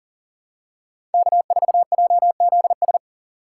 Morse code, heard as a keyed tone, about 40 wpm, 700 Hz. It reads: K4JZS
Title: USA Call Signs - 5 Characters - 4x Character Spacing - 40wpm